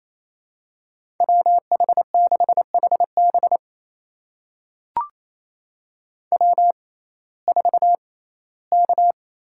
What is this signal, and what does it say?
Morse code, heard as a keyed tone, about 28 wpm, spaced as for 17 wpm, 700 Hz. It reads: WH6HB W4K
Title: USA Call Signs - 5 Characters - 6x Character Spacing - 28wpm